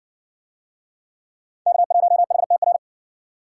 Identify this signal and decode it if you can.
Morse code, heard as a keyed tone, about 60 wpm, 700 Hz. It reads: K1HTR